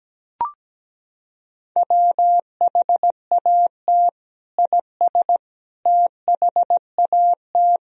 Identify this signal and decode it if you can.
Morse code, heard as a keyed tone, about 17 wpm, 700 Hz. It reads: WHAT IS THAT
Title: Sentences from Top 1000 Words - Review Part 1 - 17wpm